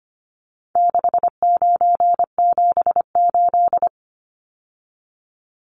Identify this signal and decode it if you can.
Morse code, heard as a keyed tone, about 25 wpm, 700 Hz. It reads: 6978
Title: Numbers - 4-Digit 25wpm